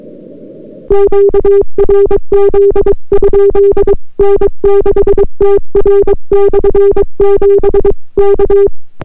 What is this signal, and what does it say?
Morse code, heard as a keyed tone, about 22 wpm, 400 Hz. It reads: QRZ? N6TR/7 K